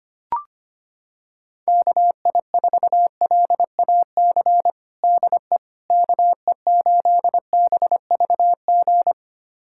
Morse code, heard as a keyed tone, about 25 wpm, 700 Hz. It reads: KI4LAC DE KE8BVG
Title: USA Callsign DE Callsign - 6 Characters - 25wpm